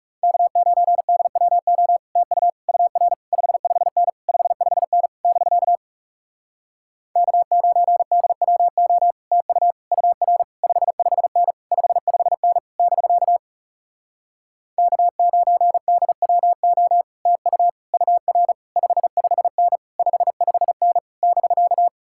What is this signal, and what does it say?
Morse code, heard as a keyed tone, about 40 wpm, 700 Hz. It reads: K9DWO TU UR 55N 55N <BK> K9DWO TU UR 55N 55N <BK> K9DWO TU UR 55N 55N <BK>